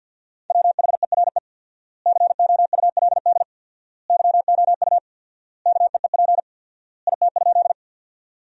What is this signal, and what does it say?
Morse code, heard as a keyed tone, about 50 wpm, 700 Hz. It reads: WHERE COULD YOU KEEP IT?
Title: 4x Word-Spacing - Sentences from Top 300 Words - 50wpm